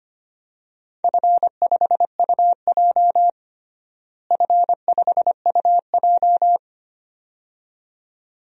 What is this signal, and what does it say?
Morse code, heard as a keyed tone, about 25 wpm, 700 Hz. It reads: F5UJ F5UJ
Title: DXCC-100 Call Signs - 4 Characters - Repeated 2x - 25wpm